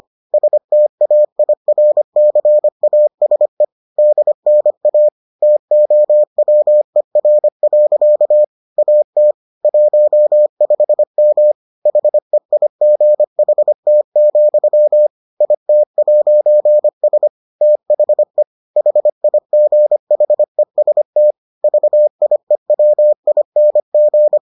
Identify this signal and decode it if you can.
Morse code, heard as a keyed tone, about 25 wpm, 600 Hz. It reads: STAIRCASE DNA TOWER. AT 15M HEIGHT, IT'S THE HIGHEST VIEWING